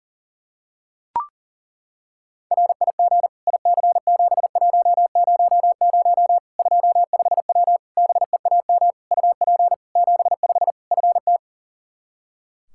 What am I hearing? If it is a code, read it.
Morse code, heard as a keyed tone, about 37 wpm, 700 Hz. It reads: RIG IC7100 25W BEAM UP 75 FT
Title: Practice QSO - Exchange 2 - Info - 40wpm